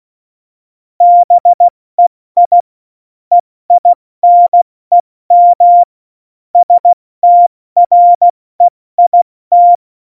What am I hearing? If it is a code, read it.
Morse code, heard as a keyed tone, about 16 wpm, 700 Hz. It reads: BEI EINEM STREIT
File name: Morsekurs / Witze / Witz 17.Dezember 2019, 70 F80